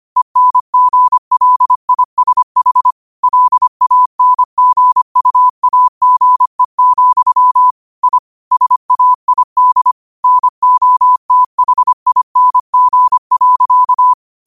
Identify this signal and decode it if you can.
Morse code, heard as a keyed tone, about 25 wpm, 1000 Hz. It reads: ENGLISH LANGUAGE, I SAID NOTHING.